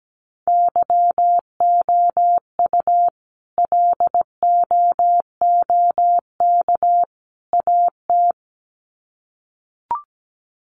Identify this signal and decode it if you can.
Morse code, heard as a keyed tone, about 17 wpm, 700 Hz. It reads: YOU LOOK AT
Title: Sets of 3 Words 17wpm